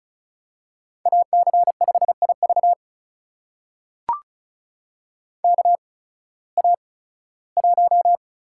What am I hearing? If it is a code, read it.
Morse code, heard as a keyed tone, about 34 wpm, spaced as for 18 wpm, 700 Hz. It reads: AC5IV KA1
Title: USA Call Signs - 8x Character Spacing - 35wpm